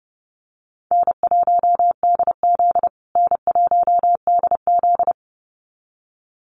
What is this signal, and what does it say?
Morse code, heard as a keyed tone, about 30 wpm, 700 Hz. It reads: N1DZ N1DZ
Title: USA Call Signs Repeated 2x 30wpm